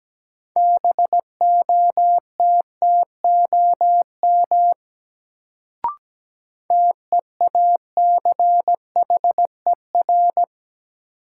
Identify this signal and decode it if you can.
Morse code, heard as a keyed tone, about 17 wpm, 700 Hz. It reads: BOTTOM TEACHER
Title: Sentences from Top 900 Words - New Word Drill - 17wpm